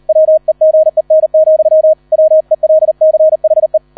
Morse code, heard as a keyed tone, about 38 wpm, 630 Hz. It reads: WEGEN, WELCHE